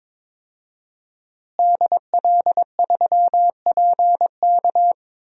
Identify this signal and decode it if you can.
Morse code, heard as a keyed tone, about 22 wpm, 700 Hz. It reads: DL3PK
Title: DXCC-100 Call Signs - 5 Characters - Repeated 2x - 22wpm